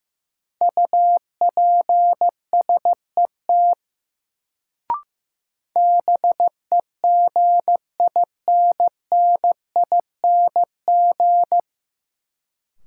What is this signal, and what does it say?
Morse code, heard as a keyed tone, about 15 wpm, 700 Hz. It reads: UPSET BEGINNING
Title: Sentences from Top 800 Words - New Word Drill - 15wpm